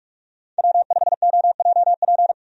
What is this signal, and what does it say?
Morse code, heard as a keyed tone, about 46 wpm, 700 Hz. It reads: W5OJP
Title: USA Call Signs - 5 Characters - 45wpm